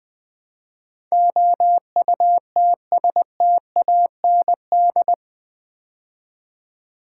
Morse code, heard as a keyed tone, about 20 wpm, 700 Hz. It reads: OUTSTAND
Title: Build A Word - 4 Letters At A Time - 11-Letter Words - 20wpm